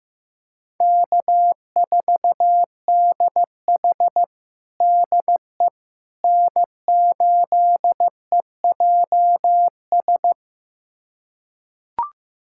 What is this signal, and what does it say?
Morse code, heard as a keyed tone, about 16 wpm, 700 Hz. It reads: K4DH DE N8EJS
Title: USA Callsign DE Callsign 15wpm